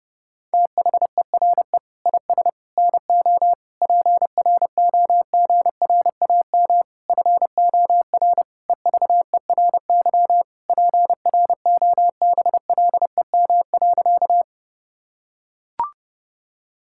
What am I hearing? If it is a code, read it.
Morse code, heard as a keyed tone, about 32 wpm, 700 Hz. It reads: THERE IS NO PROGRAM FOR EVERY PROBLEM.
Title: Sentences from Top 1000 Words - Review Part 5 - 30wpm